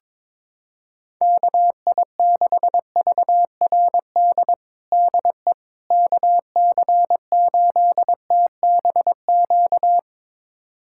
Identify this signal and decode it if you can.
Morse code, heard as a keyed tone, about 22 wpm, 700 Hz. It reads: KI6VRD DE KC8TBQ
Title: USA Callsign DE Callsign 22wpm